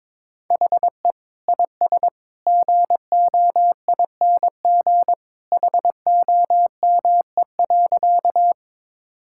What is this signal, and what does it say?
Morse code, heard as a keyed tone, about 22 wpm, 700 Hz. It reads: HE IS GOING HOME.
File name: practice / 22wpm / Sentences with 2-Word Phrases - 22wpm